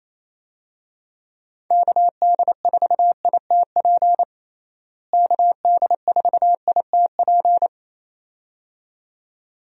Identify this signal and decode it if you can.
Morse code, heard as a keyed tone, about 27 wpm, 700 Hz. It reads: KD4ITP KD4ITP
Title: USA Call Signs - 6 Characters - Repeated 2x - 28wpm